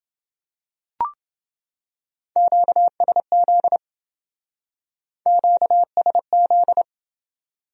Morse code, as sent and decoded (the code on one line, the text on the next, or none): --.- ... --.. / --.- ... --..
QSZ QSZ